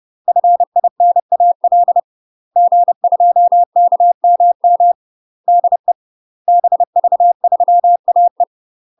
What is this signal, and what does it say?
Morse code, heard as a keyed tone, about 30 wpm, 700 Hz. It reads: FINAL G2KMM DE BV3AE
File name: UK MASTER QSOs TRAINING / 30WPM / UK MASTER QSOs 0023